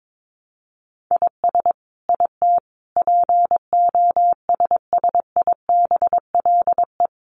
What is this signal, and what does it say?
Morse code, heard as a keyed tone, about 22 wpm, 700 Hz. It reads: IS IT POSSIBLE
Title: Sentences from Top 1200 Words 22wpm